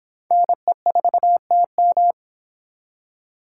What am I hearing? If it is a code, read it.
Morse code, heard as a keyed tone, about 26 wpm, 700 Hz. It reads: NE4TM